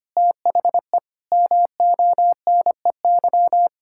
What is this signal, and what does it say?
Morse code, heard as a keyed tone, about 25 wpm, 700 Hz. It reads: THE MONEY
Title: Pangram Challenge v2 - 25wpm